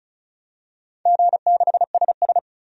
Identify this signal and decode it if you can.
Morse code, heard as a keyed tone, about 34 wpm, 700 Hz. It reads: G6SS
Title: DXCC-100 Call Signs - 4 Characters - 35wpm